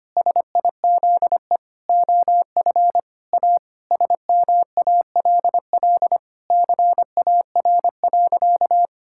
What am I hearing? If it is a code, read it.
Morse code, heard as a keyed tone, about 25 wpm, 700 Hz. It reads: SIZE OF A SMALL CAR.
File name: practice / 25wpm / Sentences with 4-Word Phrases - 25wpm